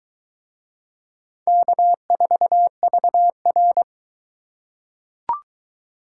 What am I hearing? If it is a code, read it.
Morse code, heard as a keyed tone, about 20 wpm, 700 Hz. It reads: K4VR